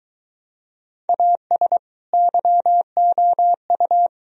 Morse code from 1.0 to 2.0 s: AS

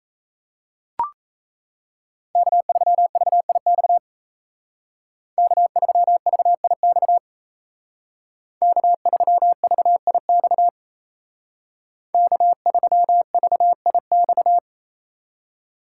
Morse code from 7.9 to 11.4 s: K3VIX

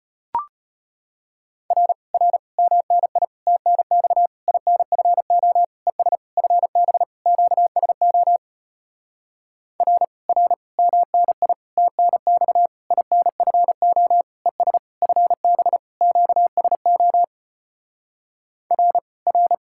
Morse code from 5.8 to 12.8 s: ES FB QSO R R MNI TNX

